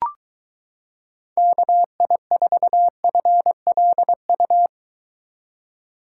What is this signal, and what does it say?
Morse code, heard as a keyed tone, about 22 wpm, 700 Hz. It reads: KI4FLU